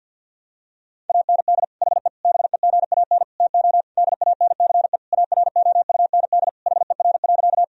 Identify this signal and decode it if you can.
Morse code, heard as a keyed tone, about 50 wpm, 700 Hz. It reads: AND HE BEGAN TO DANCE AROUND HER.
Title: Sentences from Top 1000 Words - Review Part 4 - 50wpm